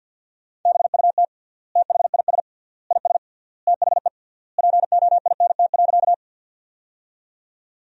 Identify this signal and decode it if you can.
Morse code, heard as a keyed tone, about 53 wpm, 700 Hz. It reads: BUT THIS IS THE POINT.
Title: 3x Word-Spacing - Sentences from Top 200 Words - 50wpm